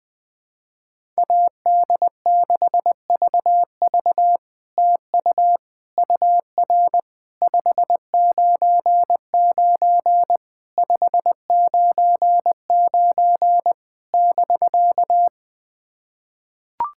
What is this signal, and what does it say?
Morse code, heard as a keyed tone, about 20 wpm, 700 Hz. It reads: AD6VV TU UR 599 599 <BK>
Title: SOTA QSO Practice 20wpm